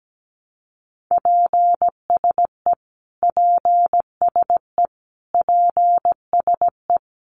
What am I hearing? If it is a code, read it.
Morse code, heard as a keyed tone, about 17 wpm, 700 Hz. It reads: PSE PSE PSE
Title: Abbreviations Repeated 3x - 17wpm